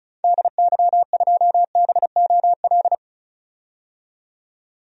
Morse code, heard as a keyed tone, about 36 wpm, 700 Hz. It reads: DY2BOL
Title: DXCC-100 Call Signs - 6 Characters - 35wpm